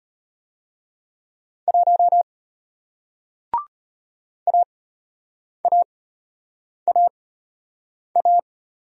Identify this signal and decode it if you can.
Morse code, heard as a keyed tone, about 33 wpm, spaced as for 8 wpm, 700 Hz. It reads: 1 AAAA